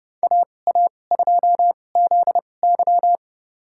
Morse code, heard as a keyed tone, about 31 wpm, spaced as for 22 wpm, 700 Hz. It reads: AA2ZY
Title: USA Call Signs - 5 Characters - 2x Character Spacing - 30wpm